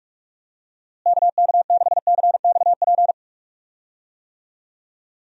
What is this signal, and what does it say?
Morse code, heard as a keyed tone, about 42 wpm, 700 Hz. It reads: KK6CXP
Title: USA Call Signs - 6 Characters - Repeated 2x - 45wpm